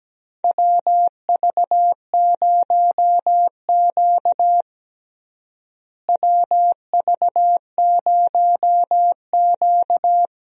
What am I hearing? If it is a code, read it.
Morse code, heard as a keyed tone, about 17 wpm, 700 Hz. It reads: WV0Q WV0Q
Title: USA Call Signs - 4 Characters - Repeated 3x - 17wpm